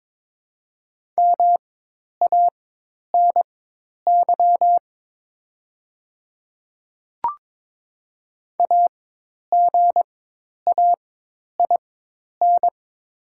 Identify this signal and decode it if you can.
Morse code, heard as a keyed tone, about 22 wpm, spaced as for 10 wpm, 690 Hz. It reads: MANY AGAIN
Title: 4x Character Spacing - Top 200 Words - 22wpm